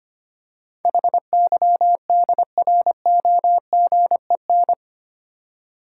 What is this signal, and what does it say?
Morse code, heard as a keyed tone, about 25 wpm, 700 Hz. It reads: HYDROGEN
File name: Eight Letter Words - 25wpm